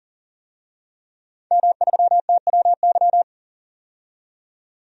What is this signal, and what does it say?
Morse code, heard as a keyed tone, about 40 wpm, 700 Hz. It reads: M3TWY
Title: DXCC-100 Call Signs - 5 Characters - 40wpm